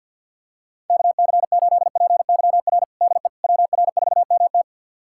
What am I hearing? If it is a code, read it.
Morse code, heard as a keyed tone, about 53 wpm, 700 Hz. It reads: KC8PYR DE WR4MT
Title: USA Callsign DE Callsign 50wpm